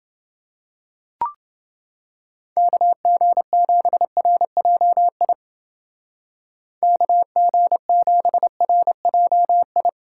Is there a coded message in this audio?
Yes: Morse code from 2.6 to 9.9 s, heard as a keyed tone, about 29 wpm, 700 Hz. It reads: KG7RJI KG7RJI